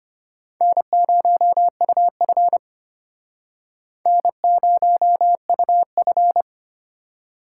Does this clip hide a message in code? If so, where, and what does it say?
Yes: Morse code from 0.6 to 6.4 s, heard as a keyed tone, about 31 wpm, 700 Hz. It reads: N0UF N0UF